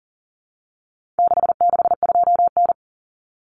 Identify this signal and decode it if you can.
Morse code, heard as a keyed tone, about 40 wpm, 700 Hz. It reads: 662N
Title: Common Cut Numbers - 4-Digit 40wpm